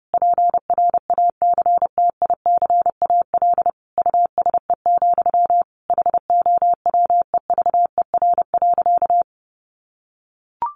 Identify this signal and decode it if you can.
Morse code, heard as a keyed tone, about 30 wpm, 700 Hz. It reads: PRACTICAL USE, HOWEVER.
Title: Sentences from Top 1400 Words 30wpm